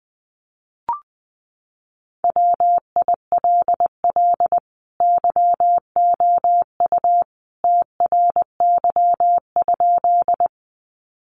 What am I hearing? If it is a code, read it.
Morse code, heard as a keyed tone, about 20 wpm, 700 Hz. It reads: WILL YOU TRY?
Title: Sentences from Top 200 Words 20wpm